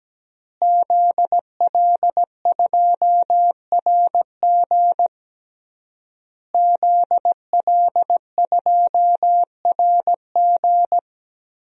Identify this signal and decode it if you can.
Morse code, heard as a keyed tone, about 17 wpm, 700 Hz. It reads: ZL2RG ZL2RG
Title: DXCC-100 Call Signs - 5 Characters - Repeated 3x - 17wpm